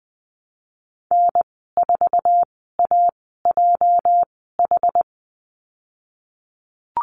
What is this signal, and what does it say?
Morse code, heard as a keyed tone, about 19 wpm, spaced as for 14 wpm, 700 Hz. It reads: N4AJH
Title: 238 Lesson 38g - Letter q - Call Signs 20wpm